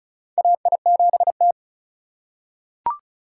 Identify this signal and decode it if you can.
Morse code, heard as a keyed tone, about 36 wpm, 700 Hz. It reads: AI7T
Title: USA Call Signs - 3x Character Spacing - 35wpm